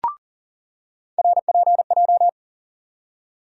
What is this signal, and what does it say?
Morse code, heard as a keyed tone, about 40 wpm, 700 Hz. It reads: RPJ
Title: USA Call Sign Postfixes - 3 Letters 40wpm